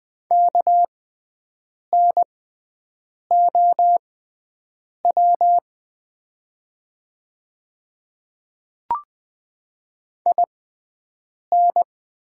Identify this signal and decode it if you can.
Morse code, heard as a keyed tone, about 20 wpm, spaced as for 7 wpm, 700 Hz. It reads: KNOW IN
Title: 6x Character Spacing - Top 100 Words - 20wpm